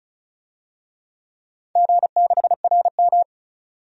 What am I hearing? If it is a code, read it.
Morse code, heard as a keyed tone, about 36 wpm, 700 Hz. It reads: G6RM
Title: DXCC-100 Call Signs - 4 Characters - 35wpm